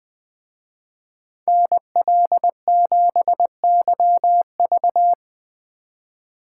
Morse code, heard as a keyed tone, about 21 wpm, 700 Hz. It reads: NL7YV
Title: Speed Racing - USA 2x2 Call Signs 17wpm